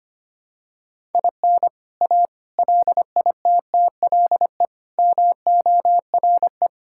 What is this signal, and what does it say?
Morse code, heard as a keyed tone, about 25 wpm, 700 Hz. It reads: IN A LITTLE MORE
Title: Sentences from Top 1000 Words - Review Part 5 - 25wpm